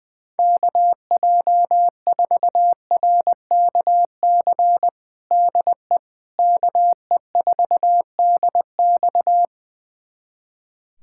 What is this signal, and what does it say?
Morse code, heard as a keyed tone, about 20 wpm, 700 Hz. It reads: KJ4RKC DE KE4DX
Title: USA Callsign DE Callsign 20wpm